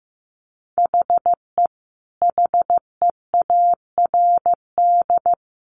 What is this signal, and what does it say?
Morse code, heard as a keyed tone, about 15 wpm, 700 Hz. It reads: HE HEARD
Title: A Sound Of Thunder 15wpm